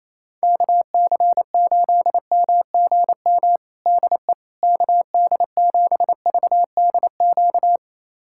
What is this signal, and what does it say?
Morse code, heard as a keyed tone, about 28 wpm, 700 Hz. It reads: KC8MGM DE KD7VDQ
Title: USA Callsign DE Callsign - 6 Characters - 28wpm